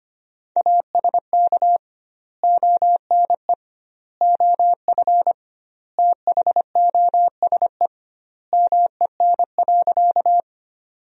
Morse code, heard as a keyed tone, about 25 wpm, 700 Hz. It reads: ASK ONE OF THOSE MEN.
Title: 2x Word-Spacing - Sentences from Top 300 Words - 25wpm